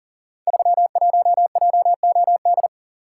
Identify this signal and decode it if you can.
Morse code, heard as a keyed tone, about 39 wpm, 700 Hz. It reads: 31JOD
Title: Code Groups - 5 Letter:Numbers - 40wpm